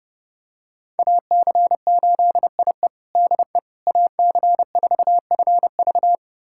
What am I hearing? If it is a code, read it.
Morse code, heard as a keyed tone, about 32 wpm, 700 Hz. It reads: AC8IE DE AC4FV
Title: USA Callsign DE Callsign - 5 Characters - 30wpm